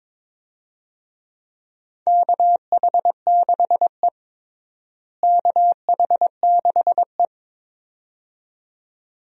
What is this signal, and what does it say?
Morse code, heard as a keyed tone, about 22 wpm, 700 Hz. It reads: KH6E KH6E
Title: DXCC-100 Call Signs - 4 Characters - Repeated 2x - 22wpm